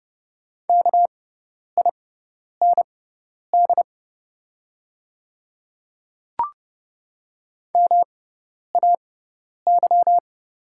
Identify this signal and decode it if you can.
Morse code, heard as a keyed tone, about 30 wpm, spaced as for 10 wpm, 700 Hz. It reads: KIND MAY
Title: 6x Character Spacing - Top 200 Words - 30wpm